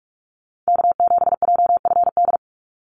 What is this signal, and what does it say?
Morse code, heard as a keyed tone, about 44 wpm, 700 Hz. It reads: K7JFD
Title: NA SOTA Activator Callsigns 45wpm